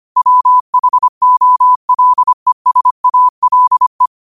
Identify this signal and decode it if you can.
Morse code, heard as a keyed tone, about 25 wpm, 1000 Hz. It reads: WHOLESALE
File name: qotdfiles / 25wpm